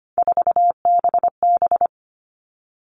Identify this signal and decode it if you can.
Morse code, heard as a keyed tone, about 25 wpm, 700 Hz. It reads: 4BB
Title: All Cut Numbers - 3-Digit 25wpm